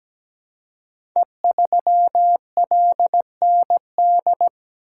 Morse code, heard as a keyed tone, about 17 wpm, 700 Hz. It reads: E3LND
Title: Code Groups - 5 Letter:Numbers - 17wpm